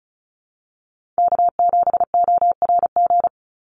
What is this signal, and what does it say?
Morse code, heard as a keyed tone, about 34 wpm, 700 Hz. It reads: K7ORG
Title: NA SOTA Chaser Callsigns 35wpm